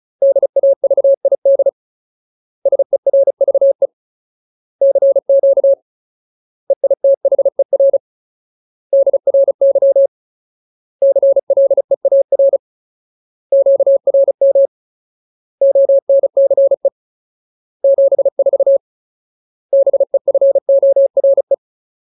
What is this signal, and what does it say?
Morse code, heard as a keyed tone, about 34 wpm, 550 Hz. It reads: DAVID SERVE CQ EITHER DRY CLEAR QRM ONCE 74 BEFORE
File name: Ultimate Word List / 35WPM Ultimate